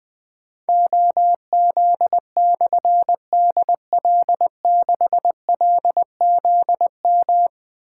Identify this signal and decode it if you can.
Morse code, heard as a keyed tone, about 20 wpm, 700 Hz. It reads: OZ/DL6LZM
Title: Difficult Call Signs 20wpm